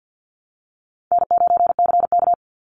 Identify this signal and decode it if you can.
Morse code, heard as a keyed tone, about 50 wpm, 700 Hz. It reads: N9CK
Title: Easy Call Signs Repeated 2x - 50wpm